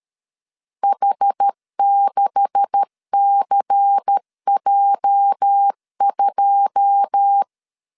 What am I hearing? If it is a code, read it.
Morse code, heard as a keyed tone, about 13 wpm, 800 Hz. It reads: H6CJ2
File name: CW / DET / 1 9 1500C S C 12wpm 3811